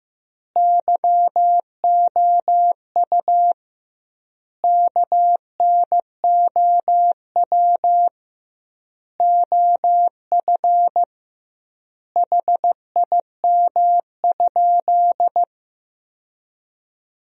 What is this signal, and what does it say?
Morse code, heard as a keyed tone, about 16 wpm, 700 Hz. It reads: YOU KNOW OF HIM?
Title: 2x Word-Spacing - Sentences from Top 100 Words - 15wpm